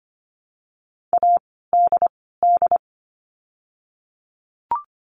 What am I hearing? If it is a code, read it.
Morse code, heard as a keyed tone, about 25 wpm, 700 Hz. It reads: A D D
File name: course / 25:10wpm / 094 Lesson 16f - Letter c - Words 25wpm